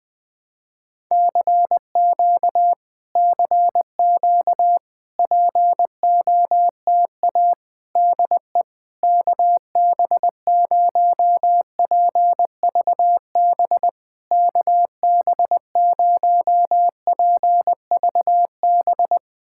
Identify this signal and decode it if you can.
Morse code, heard as a keyed tone, about 20 wpm, 700 Hz. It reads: CQ CQ POTA DE KB0PVB KB0PVB
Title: POTA QSO Practice - 20wpm